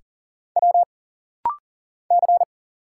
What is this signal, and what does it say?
Morse code, heard as a keyed tone, about 40 wpm, 700 Hz. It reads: W C
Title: Single Letter-Number Alt - Rapid-Fire 40wpm